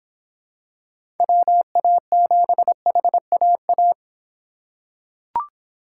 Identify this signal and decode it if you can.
Morse code, heard as a keyed tone, about 25 wpm, 700 Hz. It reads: WA7HAA